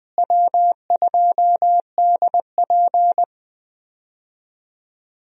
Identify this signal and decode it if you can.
Morse code, heard as a keyed tone, about 20 wpm, 700 Hz. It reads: W2DP
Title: USA Call Signs - 4 Characters - Repeated 3x - 20wpm